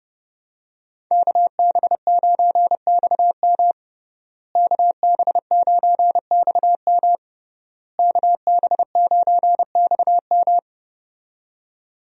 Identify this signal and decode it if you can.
Morse code, heard as a keyed tone, about 30 wpm, 700 Hz. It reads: KB9XM KB9XM KB9XM
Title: USA Call Signs - 5 Characters - Repeated 3x - 30wpm